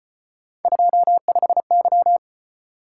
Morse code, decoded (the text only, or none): 25Y